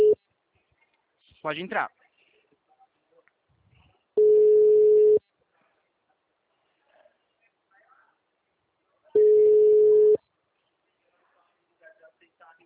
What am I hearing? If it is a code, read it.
Ringback tone (Europe, 425 Hz), with speech.
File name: out- -2006- - - .82842